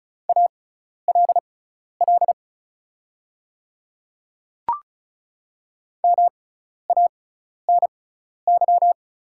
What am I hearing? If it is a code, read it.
Morse code, heard as a keyed tone, about 36 wpm, spaced as for 12 wpm, 700 Hz. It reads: ALL MANY
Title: 6x Character Spacing - Top 100 Words - 35wpm